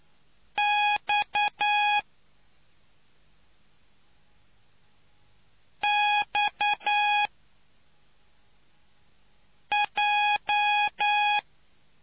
Morse code, heard as a keyed tone, about 9 wpm, spaced as for 3 wpm, 830 Hz. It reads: X XJ